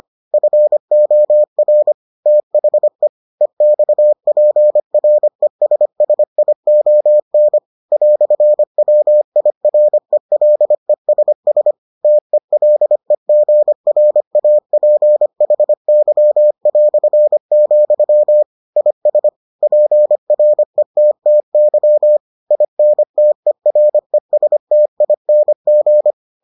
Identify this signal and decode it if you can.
Morse code, heard as a keyed tone, about 25 wpm, 600 Hz. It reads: FOR THE EXPRESSION "WIRELESS TELEGRAPHY", IS PRETTY INTERESTING